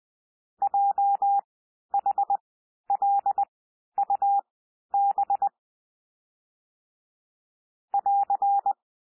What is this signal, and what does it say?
Morse code, heard as a keyed tone, about 20 wpm, spaced as for 11 wpm, 800 Hz. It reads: JHLUB <AR>
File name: cwmp3 / L3 SOUND LETTERS 20 wpm